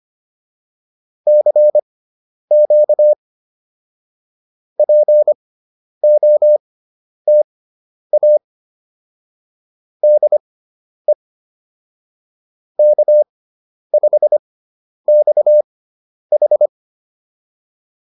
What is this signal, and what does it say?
Morse code, heard as a keyed tone, about 25 wpm, spaced as for 10 wpm, 600 Hz. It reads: CQ POTA DE K5XH
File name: POTA205 10